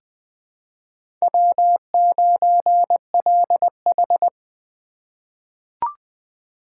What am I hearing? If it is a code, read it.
Morse code, heard as a keyed tone, about 20 wpm, 700 Hz. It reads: W9LH
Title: USA Call Signs - 4x Character Spacing - 20wpm